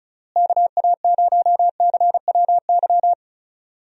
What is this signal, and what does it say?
Morse code, heard as a keyed tone, about 36 wpm, 700 Hz. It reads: KA0CWY
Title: USA Call Signs - 6 Characters - Repeated 3x - 35wpm